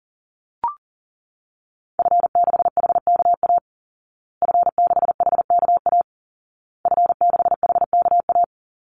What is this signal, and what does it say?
Morse code, heard as a keyed tone, about 40 wpm, 700 Hz. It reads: F6HKA F6HKA F6HKA
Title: Call Signs Repeated 2x - 40wpm